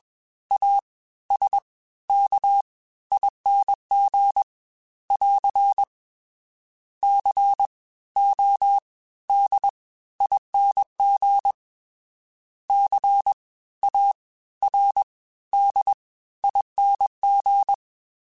Morse code, heard as a keyed tone, about 21 wpm, 800 Hz. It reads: A S K ING <AR> C O D ING C A R D ING
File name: Special INGs-mp3s / Special INGs-mp3s / Fast ing @20